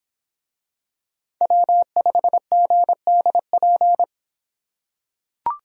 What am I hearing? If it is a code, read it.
Morse code, heard as a keyed tone, about 26 wpm, 700 Hz. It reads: W5GDP